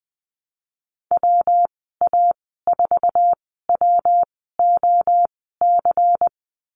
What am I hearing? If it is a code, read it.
Morse code, heard as a keyed tone, about 20 wpm, spaced as for 14 wpm, 700 Hz. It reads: WA4WOC
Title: 193 Lesson 31g - Letter v - Call Signs 20wpm